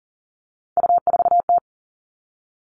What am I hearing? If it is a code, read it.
Morse code, heard as a keyed tone, about 40 wpm, 700 Hz. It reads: U4T
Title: All Cut Numbers - 3-Digit 40wpm